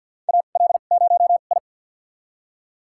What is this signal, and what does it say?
Morse code, heard as a keyed tone, about 48 wpm, spaced as for 35 wpm, 700 Hz. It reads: AL0I
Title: USA Call Signs - 4 Characters - 2x Character Spacing - 50wpm